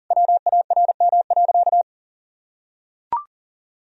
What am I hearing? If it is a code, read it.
Morse code, heard as a keyed tone, about 40 wpm, 700 Hz. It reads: WARM.
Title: Sentences from Top 1000 Words - Review Part 2 - 40wpm